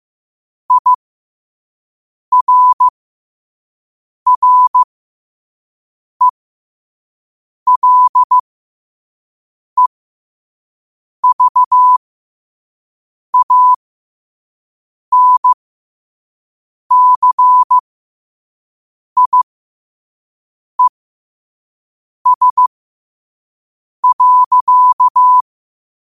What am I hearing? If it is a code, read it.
Morse code, heard as a keyed tone, about 15 wpm, spaced as for 5 wpm, 1000 Hz. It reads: IRRELEVANCIES.
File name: qotdfiles / 5wpm